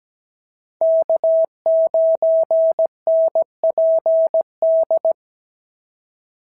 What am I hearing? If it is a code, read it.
Morse code, heard as a keyed tone, about 16 wpm, 650 Hz. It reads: K9NPD
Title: SOTA QSO Practice 17wpm